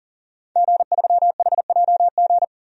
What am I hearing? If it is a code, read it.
Morse code, heard as a keyed tone, about 40 wpm, 700 Hz. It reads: G3HJG
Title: DXCC-100 Call Signs - 5 Characters - 40wpm